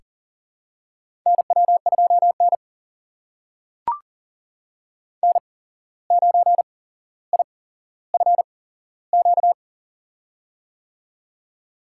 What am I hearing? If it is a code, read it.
Morse code, heard as a keyed tone, about 40 wpm, spaced as for 16 wpm, 700 Hz. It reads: NW2N N9IFQ
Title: USA Call Signs - 8x Character Spacing - 40wpm